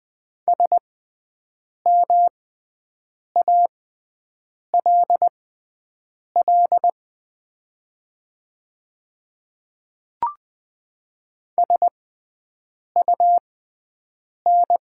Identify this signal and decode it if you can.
Morse code, heard as a keyed tone, about 20 wpm, spaced as for 7 wpm, 700 Hz. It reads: SMALL SUN
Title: 6x Character Spacing - Top 300 Words - 20wpm